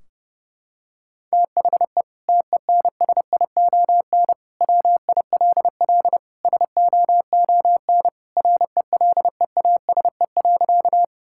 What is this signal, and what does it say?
Morse code, heard as a keyed tone, about 30 wpm, 700 Hz. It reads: THE TENSION WILL SOON RELEASE.
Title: Sentences from Top 1600 Words 30wpm